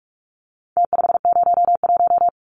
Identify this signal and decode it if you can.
Morse code, heard as a keyed tone, about 44 wpm, 700 Hz. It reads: T501
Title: Numbers - Contest Style - 4-Digit 45wpm